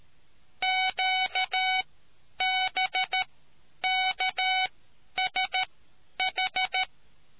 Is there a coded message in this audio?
Yes: Morse code from 0.6 to 6.8 s, heard as a keyed tone, about 13 wpm, spaced as for 9 wpm, 750 Hz. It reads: QBKSH